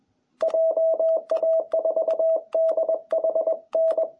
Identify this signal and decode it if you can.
Morse code, heard as a keyed tone, about 21 wpm, 660 Hz. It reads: JA4DHN
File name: jh8cbh / 2021cw